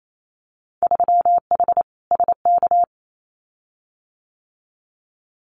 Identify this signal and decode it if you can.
Morse code, heard as a keyed tone, about 28 wpm, 700 Hz. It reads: 3H SK
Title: Winter Field Day Exchanges 28wpm